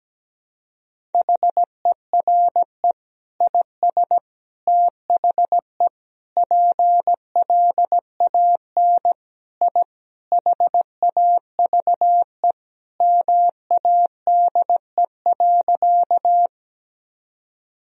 Morse code, heard as a keyed tone, about 17 wpm, 700 Hz. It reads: HERE IS THE PLAN I HAVE MADE.
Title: Sentences from Top 1000 Words - Review Part 2 - 17wpm